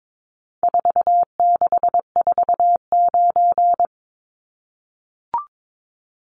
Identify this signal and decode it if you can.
Morse code, heard as a keyed tone, about 22 wpm, 700 Hz. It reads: 4649
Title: Numbers - 4-Digit 22wpm